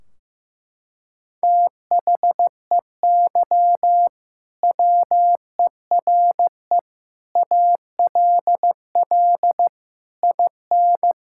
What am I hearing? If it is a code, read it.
Morse code, heard as a keyed tone, about 15 wpm, 700 Hz. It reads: THEY WERE ALL IN
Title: Sentences from Top 1600 Words 15wpm